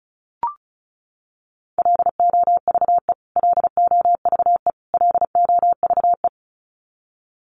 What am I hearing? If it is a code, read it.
Morse code, heard as a keyed tone, about 34 wpm, 700 Hz. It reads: LOVE LOVE LOVE
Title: Top 500 Words Repeated 35wpm